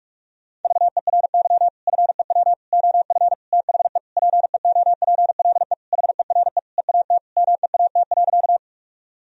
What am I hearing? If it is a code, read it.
Morse code, heard as a keyed tone, about 41 wpm, 700 Hz. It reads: VERY FEW OF THE PEOPLE HERE EAT MEAT.
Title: Sentences from Top 1000 Words - Review Part 3 - 45wpm